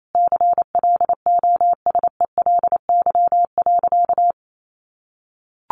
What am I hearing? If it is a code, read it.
Morse code, heard as a keyed tone, about 28 wpm, 700 Hz. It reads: CLOSELY.
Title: Sentences from Top 1400 Words 28wpm